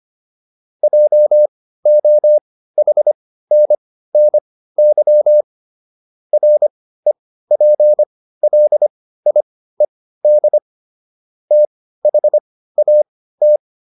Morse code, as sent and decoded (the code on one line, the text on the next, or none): .--- --- .... -. -. -.-- / .-. .--. .-.. .. -.. / - .... .- -
JOHNNY REPLIED THAT